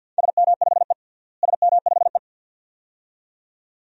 Morse code, heard as a keyed tone, about 47 wpm, 700 Hz. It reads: SM5E SM5E
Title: DXCC-100 Call Signs - 4 Characters - Repeated 2x - 50wpm